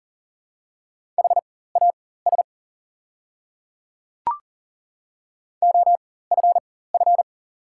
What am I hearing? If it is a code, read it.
Morse code, heard as a keyed tone, about 40 wpm, spaced as for 19 wpm, 690 Hz. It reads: HAS OFF
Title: 4x Character Spacing - Top 200 Words - 40wpm